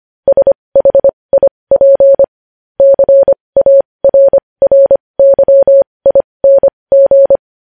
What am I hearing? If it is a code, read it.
Morse code, heard as a keyed tone, about 25 wpm, spaced as for 20 wpm, 550 Hz. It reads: SHIP CARRYING
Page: News Headlines in Morse Code at 20 WPM